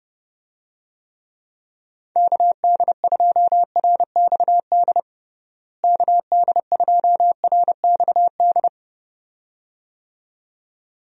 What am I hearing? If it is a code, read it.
Morse code, heard as a keyed tone, about 32 wpm, 700 Hz. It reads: KD2RXD KD2RXD
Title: USA Call Signs - 6 Characters - Repeated 2x - 30wpm